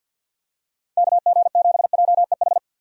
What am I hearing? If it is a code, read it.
Morse code, heard as a keyed tone, about 49 wpm, 700 Hz. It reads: KG7JEH
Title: USA Call Signs - 6 Characters - 50wpm